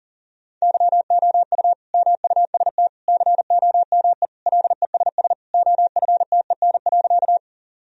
Morse code, heard as a keyed tone, about 40 wpm, 700 Hz. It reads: YOU MUST COME LESS OFTEN.
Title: Sentences from Top 1000 Words - Review Part 2 - 40wpm